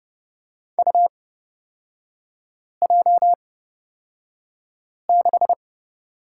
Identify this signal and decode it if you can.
Morse code, heard as a keyed tone, about 30 wpm, spaced as for 5 wpm, 700 Hz. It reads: UJ6
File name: Lesson 37e - Letter x - Three Characters 30wpm